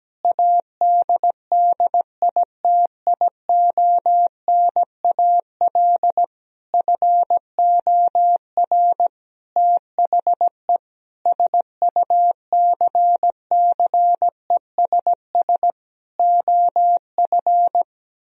Morse code, heard as a keyed tone, about 17 wpm, 700 Hz. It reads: ADDITIONAL FOR THE SUCCESS OF
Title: Sentences from Top 1000 Words - Review Part 4 - 17wpm